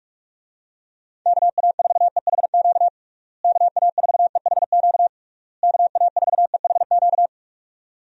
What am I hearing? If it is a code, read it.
Morse code, heard as a keyed tone, about 46 wpm, 700 Hz. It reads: KA4EHQ KA4EHQ KA4EHQ
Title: USA Call Signs - 6 Characters - Repeated 3x - 45wpm